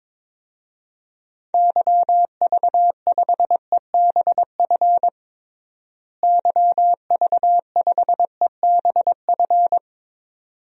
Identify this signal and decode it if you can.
Morse code, heard as a keyed tone, about 22 wpm, 700 Hz. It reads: YV5EBF YV5EBF
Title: DXCC-100 Call Signs - 6 Characters - Repeated 2x - 22wpm